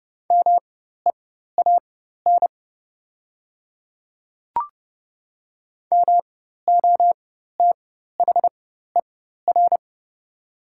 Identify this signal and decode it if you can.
Morse code, heard as a keyed tone, about 32 wpm, spaced as for 14 wpm, 700 Hz. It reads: MEAN MOTHER
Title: 4x Character Spacing - Top 200 Words - 30wpm